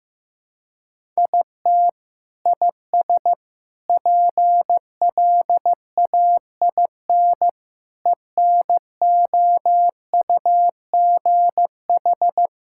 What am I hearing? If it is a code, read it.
Morse code, heard as a keyed tone, about 16 wpm, 700 Hz. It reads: IT IS PLAIN ENOUGH
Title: Sentences from Top 1000 Words - Review Part 5 - 15wpm